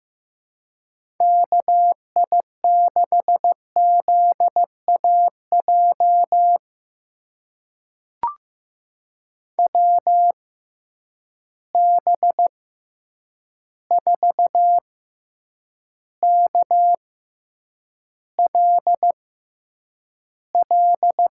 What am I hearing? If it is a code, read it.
Morse code, heard as a keyed tone, about 15 wpm, spaced as for 8 wpm, 700 Hz. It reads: KI6ZAJ WB4KLL
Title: USA Call Signs - 6 Characters - 6x Character Spacing - 15wpm